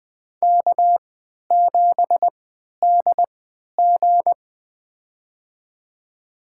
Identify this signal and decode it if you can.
Morse code, heard as a keyed tone, about 21 wpm, spaced as for 11 wpm, 700 Hz. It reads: K7DG
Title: USA Call Signs - 3x Character Spacing - 20wpm